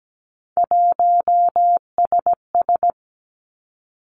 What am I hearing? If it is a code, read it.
Morse code, heard as a keyed tone, about 17 wpm, 700 Hz. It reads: 1SS
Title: Three Challenging Letter-Number Combinations 17wpm